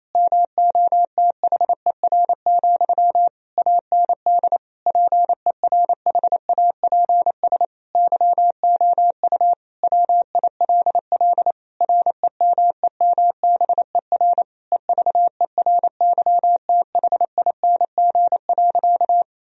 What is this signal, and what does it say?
Morse code, heard as a keyed tone, about 28 wpm, 700 Hz. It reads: MOTHER, AND PERHAPS YOU WILL REMEMBER EVERYTHING.
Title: Sentences from Top 1000 Words - Review Part 1 - 28wpm